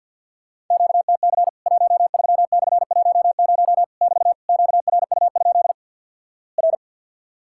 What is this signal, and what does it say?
Morse code, heard as a keyed tone, about 50 wpm, 700 Hz. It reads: QTC 13/10 <BT> QRU? R
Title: Worked All Europe DX Contest - QTC Exchange with Repeats - 50wpm